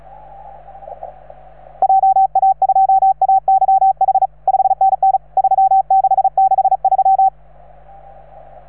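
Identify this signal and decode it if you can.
Morse code, heard as a keyed tone, about 35 wpm, 770 Hz. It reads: JA2AYH 5NN 3663
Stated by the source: Morse from station 3V9A.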